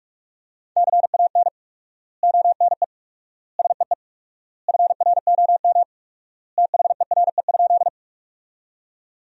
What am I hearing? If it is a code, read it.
Morse code, heard as a keyed tone, about 47 wpm, 700 Hz. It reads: CAN ONE SEE FROM THERE?
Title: 4x Word-Spacing - Sentences from Top 100 Words - 45wpm